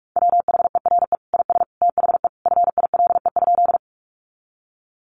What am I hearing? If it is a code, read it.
Morse code, heard as a keyed tone, about 44 wpm, 700 Hz. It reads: WHERE IS THE FILE?
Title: Sentences from Top 1200 Words 45wpm